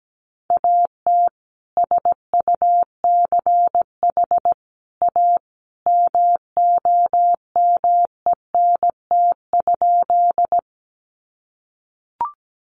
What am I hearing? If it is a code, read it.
Morse code, heard as a keyed tone, about 17 wpm, 700 Hz. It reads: AT SUCH A MOMENT?
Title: Sentences from Top 1000 Words 17wpm